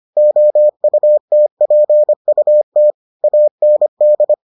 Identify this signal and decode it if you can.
Morse code, heard as a keyed tone, about 25 wpm, 600 Hz. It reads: OUTPUT AND